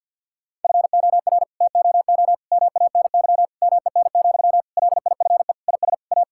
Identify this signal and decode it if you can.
Morse code, heard as a keyed tone, about 50 wpm, 700 Hz. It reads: FOR TOO MANY MEN, LIFE IS A
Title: Sentences from Top 1000 Words - Review Part 6 - 50wpm